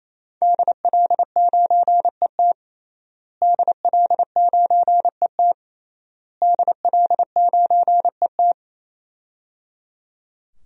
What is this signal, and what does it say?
Morse code, heard as a keyed tone, about 27 wpm, 700 Hz. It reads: DL9ET DL9ET DL9ET
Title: DXCC-100 Call Signs - 5 Characters - Repeated 3x - 28wpm